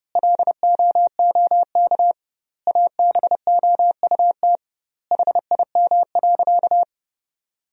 Morse code, heard as a keyed tone, about 32 wpm, 700 Hz. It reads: LOOK ABOUT HIM.
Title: 2x Word-Spacing - Sentences from Top 200 Words - 30wpm